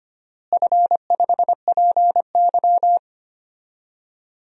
Morse code, heard as a keyed tone, about 25 wpm, 700 Hz. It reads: F5PY
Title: DXCC-100 Call Signs - 4 Characters - Repeated 2x - 25wpm